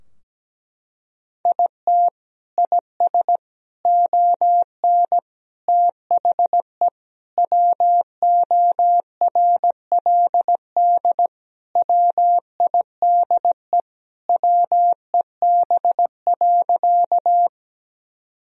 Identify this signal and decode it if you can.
Morse code, heard as a keyed tone, about 17 wpm, 700 Hz. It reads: IT IS ON THE WORLD WIDE WEB.
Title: Sentences from Top 1000 Words - Review Part 4 - 17wpm